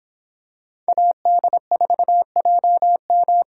Morse code, heard as a keyed tone, about 25 wpm, 700 Hz. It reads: AD4JM